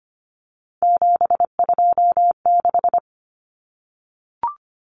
Morse code, heard as a keyed tone, about 25 wpm, 700 Hz. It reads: 726
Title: Common Cut Numbers - 3-Digit 25wpm